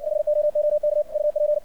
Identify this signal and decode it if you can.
Morse code, heard as a keyed tone, about 60 wpm, 600 Hz, with radio noise. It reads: HOOK UP
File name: archived / CRSnet